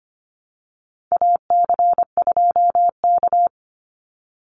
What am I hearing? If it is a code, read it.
Morse code, heard as a keyed tone, about 25 wpm, 700 Hz. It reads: AC2K
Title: Call Signs - 25wpm